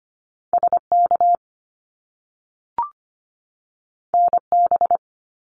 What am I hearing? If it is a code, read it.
Morse code, heard as a keyed tone, about 25 wpm, 700 Hz. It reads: SK NB
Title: Canadian Provinces and Territories 25wpm